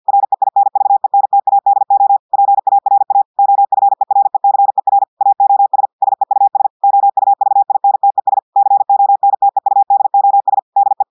Morse code, heard as a keyed tone, about 50 wpm, 800 Hz. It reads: REINVENTADO PARA OFERECER AOS SEUS OUVINTES CONTEUDOS DE
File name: portugais texte en clair 2 vitesse 50 mots minutes